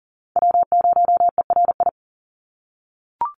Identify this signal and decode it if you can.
Morse code, heard as a keyed tone, about 40 wpm, 700 Hz. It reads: W0ERI
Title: NA SOTA Chaser Callsigns 40wpm